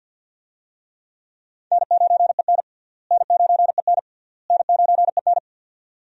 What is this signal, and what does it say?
Morse code, heard as a keyed tone, about 53 wpm, 700 Hz. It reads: N9EN N9EN N9EN
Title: USA Call Signs - 4 Characters - Repeated 3x - 50wpm